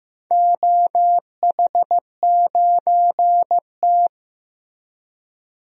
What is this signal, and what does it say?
Morse code, heard as a keyed tone, about 15 wpm, 700 Hz. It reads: OH9T
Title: DXCC-100 Call Signs - 4 Characters - Repeated 2x - 15wpm